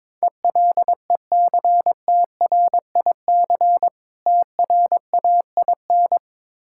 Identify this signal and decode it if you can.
Morse code, heard as a keyed tone, about 22 wpm, 690 Hz. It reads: ELECTRIC TRAIN
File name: Difficult Words with Hints - 22wpm